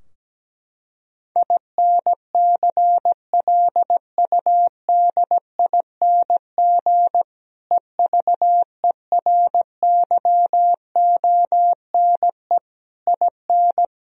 Morse code, heard as a keyed tone, about 17 wpm, 700 Hz. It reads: INCLUDING EVERYONE IN
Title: Sentences from Top 1000 Words - Review Part 4 - 17wpm